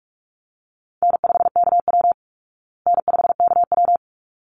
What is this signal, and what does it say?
Morse code, heard as a keyed tone, about 44 wpm, 700 Hz. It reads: N5KW N5KW
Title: Call Signs Repeated 3x - 45wpm